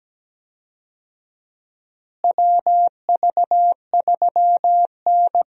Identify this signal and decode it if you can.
Morse code, heard as a keyed tone, about 17 wpm, 700 Hz. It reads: WV3N
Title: USA Call Signs - 4 Characters - Repeated 3x - 17wpm